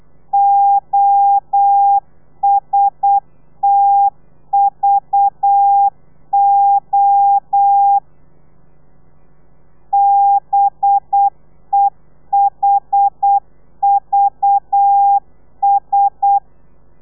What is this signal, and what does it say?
Morse code, heard as a keyed tone, about 8 wpm, 800 Hz. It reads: OSTVO BEHVS
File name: telegrafia / i2tfj / Lez4 35